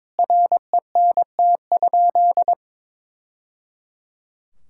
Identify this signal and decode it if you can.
Morse code, heard as a keyed tone, about 21 wpm, 700 Hz. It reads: RENT?
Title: Sentences from Top 1000 Words - Review Part 6 - 22wpm